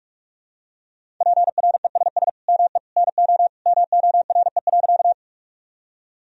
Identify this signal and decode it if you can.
Morse code, heard as a keyed tone, about 42 wpm, 700 Hz. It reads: PRESS ME NO MORE.
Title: Sentences from Top 1000 Words - Review Part 1 - 45wpm